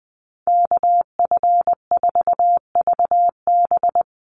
Morse code, heard as a keyed tone, about 19 wpm, 700 Hz. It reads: KF4VB
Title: USA Call Signs Repeated 2x 20wpm